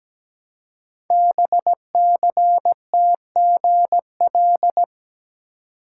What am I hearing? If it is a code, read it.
Morse code, heard as a keyed tone, about 17 wpm, 700 Hz. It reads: BCTGL
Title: Code Groups - 5 Letters - 17wpm